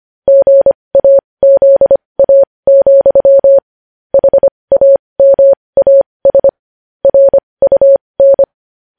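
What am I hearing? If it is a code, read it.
Morse code, heard as a keyed tone, about 25 wpm, spaced as for 20 wpm, 550 Hz. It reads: GAZA, HAMAS RUN
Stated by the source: News headlines in Morse at 20 wpm.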